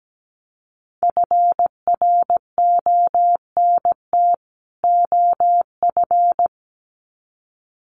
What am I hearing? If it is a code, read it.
Morse code, heard as a keyed tone, about 17 wpm, 700 Hz. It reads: FRONT OF
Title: Sets of 2 Words 17wpm